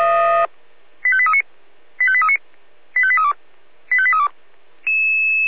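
Five-tone selective call (ZVEI), heard as quick runs of five tones, tones of about 70 ms. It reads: 85839 85839 85832 85832
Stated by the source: ZVEI five-tone selective call received by SDR.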